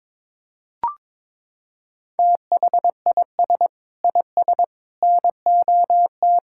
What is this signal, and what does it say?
Morse code, heard as a keyed tone, about 22 wpm, 700 Hz. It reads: THIS IS NOT
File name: practice / 22wpm / Sets of 7 Words - 22wpm